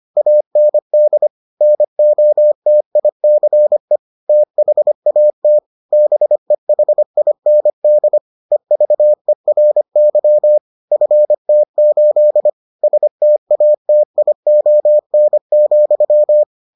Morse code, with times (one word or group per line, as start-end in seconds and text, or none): 0.2-1.3 AND
1.6-4.0 NOTICE
4.3-5.6 THAT
5.9-8.2 BEHIND
8.5-10.6 EVERY
10.9-12.5 FT8
12.8-16.4 STATION,